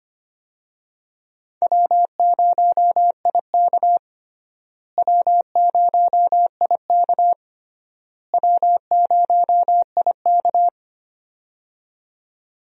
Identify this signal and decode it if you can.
Morse code, heard as a keyed tone, about 25 wpm, 700 Hz. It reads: W0IK W0IK W0IK
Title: USA Call Signs - 4 Characters - Repeated 3x - 25wpm